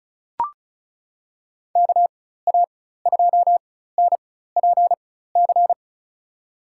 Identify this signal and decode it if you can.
Morse code, heard as a keyed tone, about 36 wpm, spaced as for 16 wpm, 700 Hz. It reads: KA2NPC
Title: USA Call Signs - 6 Characters - 4x Character Spacing - 35wpm